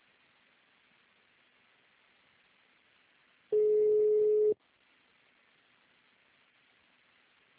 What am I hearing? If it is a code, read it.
Ringback tone (Europe, 425 Hz).
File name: out- -2003- - - .12